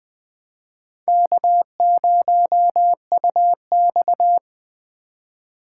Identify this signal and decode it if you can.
Morse code, heard as a keyed tone, about 21 wpm, 700 Hz. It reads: K0UX
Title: USA Call Signs - 4 Characters - 3x Character Spacing - 20wpm